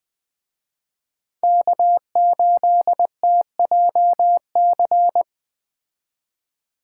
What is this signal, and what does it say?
Morse code, heard as a keyed tone, about 20 wpm, 700 Hz. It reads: K8TJC